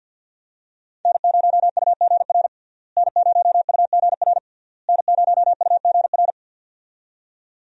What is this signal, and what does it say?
Morse code, heard as a keyed tone, about 49 wpm, 700 Hz. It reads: N0UGR N0UGR N0UGR
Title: USA Call Signs - 5 Characters - Repeated 3x - 50wpm